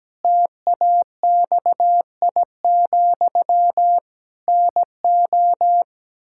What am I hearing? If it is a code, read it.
Morse code, heard as a keyed tone, about 17 wpm, 700 Hz. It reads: TAXI, NO
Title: Pangram Challenge v2 - 17wpm